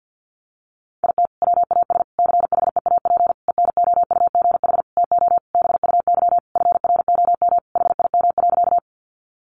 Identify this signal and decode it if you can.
Morse code, heard as a keyed tone, about 48 wpm, 700 Hz. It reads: IT WAS CHEAP ENOUGH TO BUY FROM HIM.
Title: Sentences from Top 1200 Words 50wpm